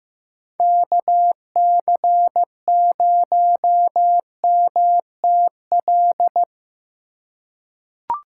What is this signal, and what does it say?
Morse code, heard as a keyed tone, about 15 wpm, 700 Hz. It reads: KC0MTL
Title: USA Call Signs - 2x Character Spacing - 15wpm